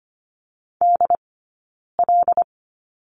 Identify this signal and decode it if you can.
Morse code, heard as a keyed tone, about 25 wpm, 700 Hz. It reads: D L
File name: Lesson 17d - Letter u - Two Characters 25wpm